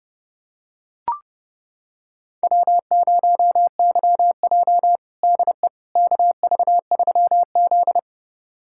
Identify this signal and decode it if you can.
Morse code, heard as a keyed tone, about 32 wpm, 700 Hz. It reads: W0YJ DE KV3Z
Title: USA Callsign DE Callsign - 4 Characters - 30wpm